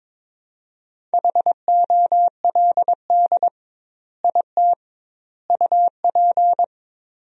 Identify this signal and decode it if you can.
Morse code, heard as a keyed tone, about 22 wpm, 700 Hz. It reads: HOLD IT UP
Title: 2x Word-Spacing - Sentences from Top 400 Words - 22wpm